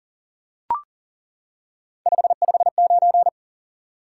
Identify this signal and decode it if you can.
Morse code, heard as a keyed tone, about 40 wpm, 700 Hz. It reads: 559
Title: Top 90 QSO Elements 40wpm